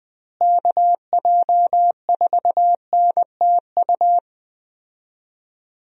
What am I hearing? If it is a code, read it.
Morse code, heard as a keyed tone, about 20 wpm, 700 Hz. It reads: KJ4NTU
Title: USA Call Signs - 6 Characters - Repeated 3x - 20wpm